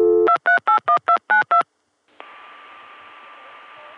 DTMF, heard as telephone touch tones, tones of about 110 ms. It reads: 2381292